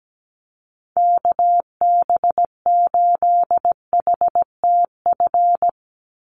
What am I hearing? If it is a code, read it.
Morse code, heard as a keyed tone, about 17 wpm, 700 Hz. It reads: KB8HTF
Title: USA Call Signs 17wpm